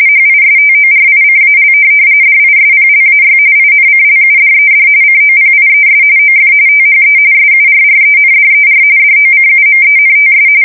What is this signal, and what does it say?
RTTY, about 45.45 baud. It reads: HISTORICAL DISPLAY INCLUDED COPIES OF THE SURRENDER DOCUMENTS
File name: fsk170 / USS-Missouri